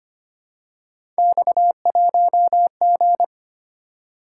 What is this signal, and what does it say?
Morse code, heard as a keyed tone, about 25 wpm, 700 Hz. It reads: X1G
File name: Three Letter-Number Combinations - 25wpm